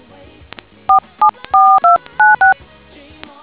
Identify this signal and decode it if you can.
DTMF, heard as telephone touch tones, tones of about 140 ms. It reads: 474296